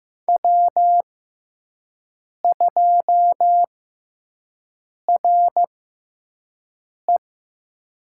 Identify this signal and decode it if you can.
Morse code, heard as a keyed tone, about 15 wpm, spaced as for 5 wpm, 700 Hz. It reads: W2RE
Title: USA Call Signs - 5 Characters - 6x Character Spacing - 15wpm